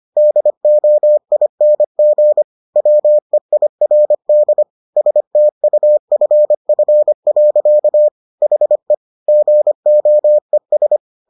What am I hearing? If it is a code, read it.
Morse code, heard as a keyed tone, about 25 wpm, 600 Hz. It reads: DOING WEIRD STUFF. HE GOES